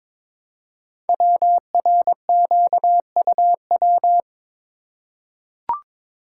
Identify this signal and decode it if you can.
Morse code, heard as a keyed tone, about 23 wpm, 700 Hz. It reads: WRQUW
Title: Code Groups - 5 Letters - 22wpm